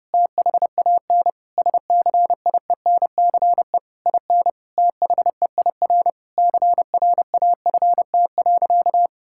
Morse code, heard as a keyed tone, about 30 wpm, 700 Hz. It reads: THAN SCIENCE IN THEIR CRAFT.
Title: Sentences from Top 1000 Words - Review Part 6 - 30wpm